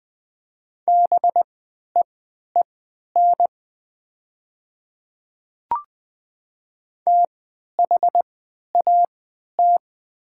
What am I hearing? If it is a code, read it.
Morse code, heard as a keyed tone, about 20 wpm, spaced as for 11 wpm, 700 Hz. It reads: BEEN THAT
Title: 3x Character Spacing - Top 100 Words - 20wpm